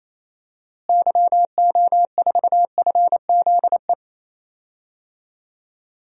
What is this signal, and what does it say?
Morse code, heard as a keyed tone, about 28 wpm, 700 Hz. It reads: YO4FZE
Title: DXCC-100 Call Signs - 6 Characters - Repeated 2x - 28wpm